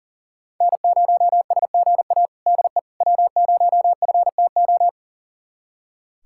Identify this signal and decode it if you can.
Morse code, heard as a keyed tone, about 40 wpm, 700 Hz. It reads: N0SGA DE W0FTO
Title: USA Callsign DE Callsign - 5 Characters - 40wpm